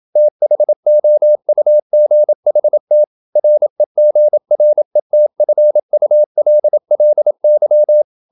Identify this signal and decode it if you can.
Morse code, heard as a keyed tone, about 28 wpm, 600 Hz. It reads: THOUGHT REGRETFULLY
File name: Never 0027wpm